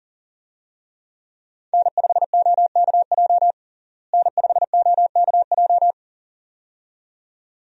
Morse code, heard as a keyed tone, about 40 wpm, 700 Hz. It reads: N5OKJ N5OKJ
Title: USA Call Signs - 5 Characters - Repeated 2x - 40wpm